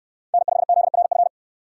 Morse code, heard as a keyed tone, about 65 wpm, 700 Hz. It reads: N5ZRF